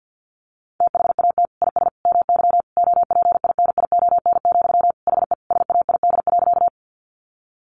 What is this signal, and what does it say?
Morse code, heard as a keyed tone, about 48 wpm, 700 Hz. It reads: THAT IS MY OPINION, HE SAID.
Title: Sentences from Top 1200 Words 50wpm